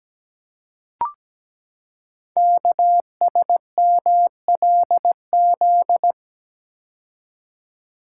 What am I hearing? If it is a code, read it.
Morse code, heard as a keyed tone, about 17 wpm, 700 Hz. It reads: KSMLZ
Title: Code Groups - 5 Letters - 17wpm